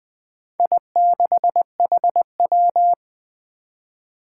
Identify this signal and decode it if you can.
Morse code, heard as a keyed tone, about 20 wpm, 700 Hz. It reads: I6HW
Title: Four Challenging Letter-Number Combinations 20wpm